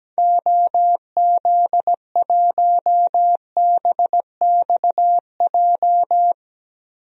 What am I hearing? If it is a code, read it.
Morse code, heard as a keyed tone, about 17 wpm, 700 Hz. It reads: OZ1BXJ
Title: DXCC-100 Call Signs - 6 Characters - Repeated 3x - 17wpm